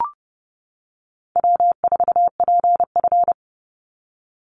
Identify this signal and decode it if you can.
Morse code, heard as a keyed tone, about 30 wpm, 700 Hz. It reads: W4PF
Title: Easy Call Signs - 30wpm